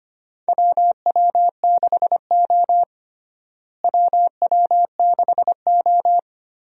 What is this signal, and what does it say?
Morse code, heard as a keyed tone, about 25 wpm, 700 Hz. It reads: WW6O WW6O
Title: USA Call Signs - 4 Characters - Repeated 3x - 25wpm